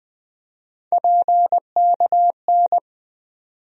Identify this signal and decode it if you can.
Morse code, heard as a keyed tone, about 20 wpm, 700 Hz. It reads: PKN
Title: USA Call Sign Postfixes - 3 Letters 20wpm